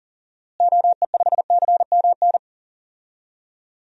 Morse code, heard as a keyed tone, about 40 wpm, 700 Hz. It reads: OE5CMN
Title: DXCC-100 Call Signs - 6 Characters - Repeated 2x - 40wpm